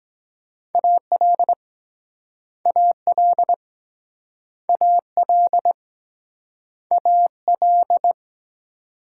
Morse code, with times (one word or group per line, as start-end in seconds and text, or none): 0.7-1.5 AL
2.7-3.5 AL
4.7-5.7 AL
6.9-8.1 AL